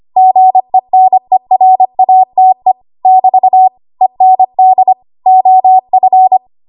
Morse code, heard as a keyed tone, about 25 wpm, 750 Hz. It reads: GENERATE <BT> END OF
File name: Archive / 25 WPM / 25WPM